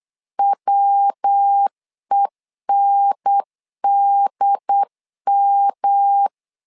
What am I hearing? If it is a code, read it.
Morse code, heard as a keyed tone, about 8 wpm, 800 Hz. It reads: WENDM
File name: LEZIONI / 4 Q1S 8wpm 2710